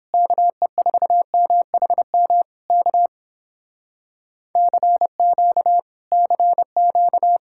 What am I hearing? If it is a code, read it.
Morse code, heard as a keyed tone, about 31 wpm, 700 Hz. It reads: KE4MHM K CQ CQ